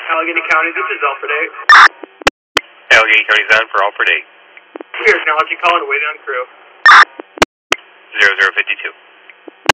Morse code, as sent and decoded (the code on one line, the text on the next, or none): none